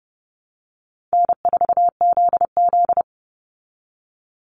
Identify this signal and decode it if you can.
Morse code, heard as a keyed tone, about 30 wpm, 700 Hz. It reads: N4ZZ
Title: Call Signs Repeated 3x - 30wpm